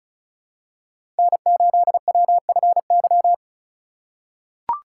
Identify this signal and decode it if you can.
Morse code, heard as a keyed tone, about 36 wpm, 700 Hz. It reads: N8WFY
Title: USA Call Signs - 5 Characters - 3x Character Spacing - 35wpm